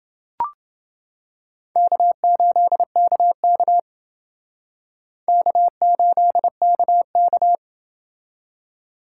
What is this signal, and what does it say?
Morse code, heard as a keyed tone, about 29 wpm, 700 Hz. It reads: K8KK K8KK